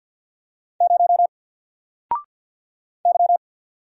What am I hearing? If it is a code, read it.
Morse code, heard as a keyed tone, about 48 wpm, 700 Hz. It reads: 0 Y
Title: Single Letter-Number 50wpm